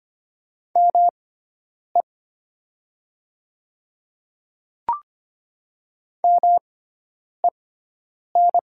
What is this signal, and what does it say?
Morse code, heard as a keyed tone, about 25 wpm, spaced as for 9 wpm, 700 Hz. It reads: ME MEN
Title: 6x Character Spacing - Top 500 Words - 25wpm